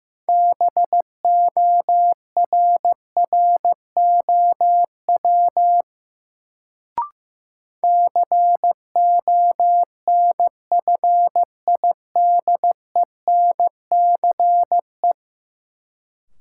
Morse code, heard as a keyed tone, about 15 wpm, 700 Hz. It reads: BORROW CONFIDENCE
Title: Sentences from Top 1500 Words - New Word Drill Alt - 15wpm